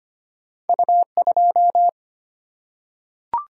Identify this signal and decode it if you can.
Morse code, heard as a keyed tone, about 25 wpm, 700 Hz. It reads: U 2
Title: Single-Letter Single-Number 25wpm